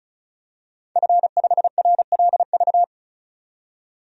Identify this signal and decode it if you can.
Morse code, heard as a keyed tone, about 36 wpm, 700 Hz. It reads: F5RLV
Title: DXCC-100 Call Signs - 5 Characters - 35wpm